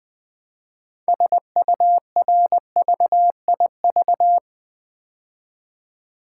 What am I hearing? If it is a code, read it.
Morse code, heard as a keyed tone, about 20 wpm, 700 Hz. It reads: SURVIV
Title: Build A Word - 3 Letters At A Time - 8-Letter Words - 20wpm